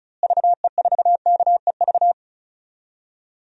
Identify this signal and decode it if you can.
Morse code, heard as a keyed tone, about 36 wpm, 700 Hz. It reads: VE4KEV
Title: DXCC-100 Call Signs - 6 Characters - Repeated 3x - 35wpm